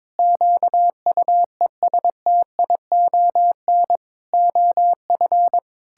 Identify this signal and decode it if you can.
Morse code, heard as a keyed tone, about 22 wpm, 700 Hz. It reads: QUESTION OF
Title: Sentences from Top 1600 Words 22wpm